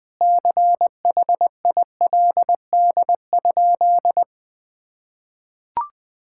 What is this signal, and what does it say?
Morse code, heard as a keyed tone, about 21 wpm, 700 Hz. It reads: CHILD?
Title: Sentences from Top 1600 Words 20wpm